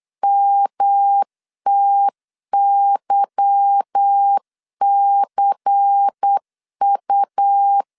Morse code, heard as a keyed tone, about 8 wpm, 800 Hz. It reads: MTYCU
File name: LEZIONI / 10 SIGNS / 10A SLASH 8wpm 3452